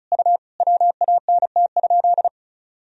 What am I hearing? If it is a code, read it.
Morse code, heard as a keyed tone, about 36 wpm, 690 Hz. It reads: U WANT?
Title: Sentences from Top 200 Words and Abbreviations 35wpm